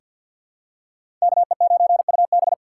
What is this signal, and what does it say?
Morse code, heard as a keyed tone, about 48 wpm, 700 Hz. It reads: KE9UB
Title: USA Call Signs - 5 Characters - 6x Character Spacing - 50wpm